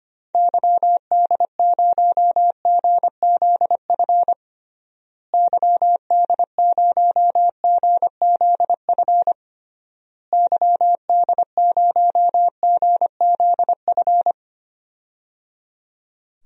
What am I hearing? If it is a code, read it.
Morse code, heard as a keyed tone, about 25 wpm, 700 Hz. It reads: YD0GZF YD0GZF YD0GZF
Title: DXCC-100 Call Signs - 6 Characters - Repeated 3x - 25wpm